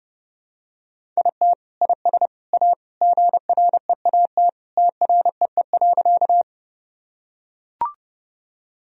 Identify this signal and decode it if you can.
Morse code, heard as a keyed tone, about 30 wpm, 700 Hz. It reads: IT IS A GREAT TREE.
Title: Sentences from Top 1000 Words - Review Part 1 - 30wpm